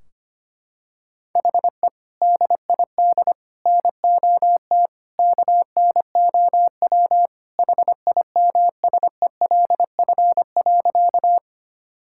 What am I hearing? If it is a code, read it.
Morse code, heard as a keyed tone, about 25 wpm, 700 Hz. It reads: HE DID NOT KNOW HIMSELF.
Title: Sentences from Top 1600 Words 25wpm